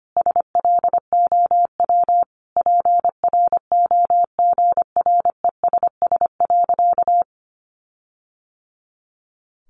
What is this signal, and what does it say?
Morse code, heard as a keyed tone, about 25 wpm, 700 Hz. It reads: SLOW PROGRESS.
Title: Sentences from Top 1400 Words 25wpm